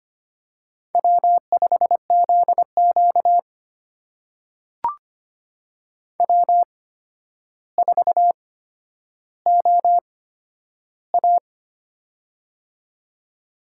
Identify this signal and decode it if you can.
Morse code, heard as a keyed tone, about 25 wpm, spaced as for 11 wpm, 700 Hz. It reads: W5ZQ W4OA
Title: USA Call Signs - 4 Characters - 8x Character Spacing - 25wpm